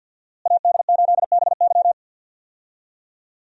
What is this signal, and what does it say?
Morse code, heard as a keyed tone, about 47 wpm, 700 Hz. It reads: AD8BY
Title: USA Call Signs - 5 Characters - Repeated 2x - 50wpm